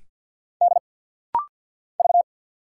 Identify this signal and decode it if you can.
Morse code, heard as a keyed tone, about 50 wpm, 700 Hz. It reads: D V
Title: Single Letter-Number - Rapid-Fire 50wpm